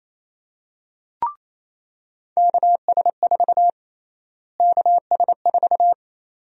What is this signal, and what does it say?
Morse code, heard as a keyed tone, about 27 wpm, 700 Hz. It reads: KS4 KS4
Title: USA Call Sign Prefixes - Repeated 2x - 28wpm